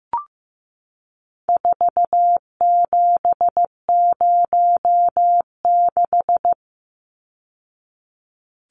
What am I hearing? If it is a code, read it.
Morse code, heard as a keyed tone, about 14 wpm, 700 Hz. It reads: 4706
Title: Numbers - 4-Digit 15wpm